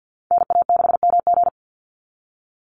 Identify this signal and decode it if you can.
Morse code, heard as a keyed tone, about 50 wpm, 700 Hz. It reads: NA6MG
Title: NA SOTA Activator Callsigns 50wpm